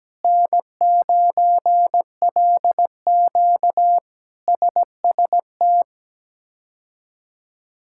Morse code, heard as a keyed tone, about 17 wpm, 700 Hz. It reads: N9LQ SST
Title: K1USN SST QSO Practice - 17wpm